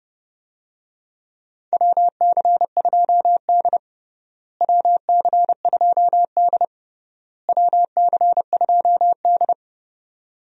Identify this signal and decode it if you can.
Morse code, heard as a keyed tone, about 32 wpm, 700 Hz. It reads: WC2D WC2D WC2D
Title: USA Call Signs - 4 Characters - Repeated 3x - 30wpm